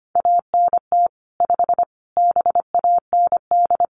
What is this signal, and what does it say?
Morse code, heard as a keyed tone, about 25 wpm, 700 Hz. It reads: ANT 5 BAND
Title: Common QSO Phrases 25wpm